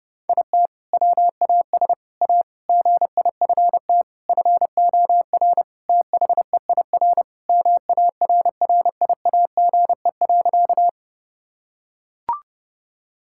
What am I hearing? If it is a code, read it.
Morse code, heard as a keyed tone, about 32 wpm, 700 Hz. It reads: IT WAS A GIFT FOR THEIR MARRIAGE.
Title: Sentences from Top 1000 Words - Review Part 6 - 30wpm